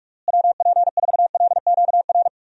Morse code, heard as a keyed tone, about 42 wpm, 700 Hz. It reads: WP4LQR
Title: USA Call Signs - 6 Characters - Repeated 3x - 45wpm